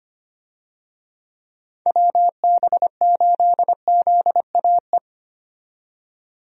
Morse code, heard as a keyed tone, about 25 wpm, 700 Hz. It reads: WB8ZAE
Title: USA Call Signs - 6 Characters - 25wpm